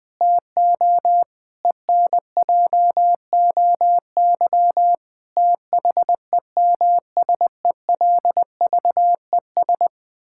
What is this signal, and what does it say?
Morse code, heard as a keyed tone, about 20 wpm, 700 Hz. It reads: TO ENJOY THEMSELVES
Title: Sentences from Top 1000 Words - Review Part 5 - 20wpm